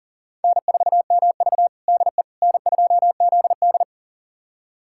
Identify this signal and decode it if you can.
Morse code, heard as a keyed tone, about 40 wpm, 700 Hz. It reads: N4MV DE N2ZD
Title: USA Callsign DE Callsign - 4 Characters - 40wpm